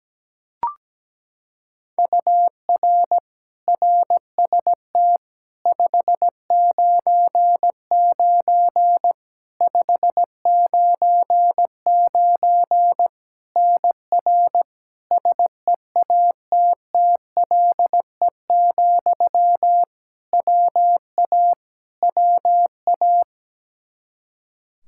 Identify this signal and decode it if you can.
Morse code, heard as a keyed tone, about 17 wpm, 700 Hz. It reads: UR RST 599 599 NR SEATTLE, WA WA
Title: Practice QSO - Exchange 1 - Beginning - 17wpm